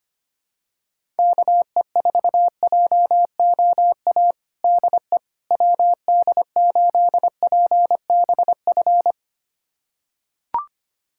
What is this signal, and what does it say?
Morse code, heard as a keyed tone, about 25 wpm, 700 Hz. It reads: KE4JOA DE WD8PBF
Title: USA Callsign DE Callsign - 6 Characters - 25wpm